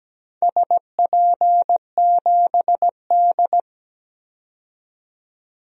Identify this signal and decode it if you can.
Morse code, heard as a keyed tone, about 17 wpm, 700 Hz. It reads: SP7D
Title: DXCC-100 Call Signs - 4 Characters - 17wpm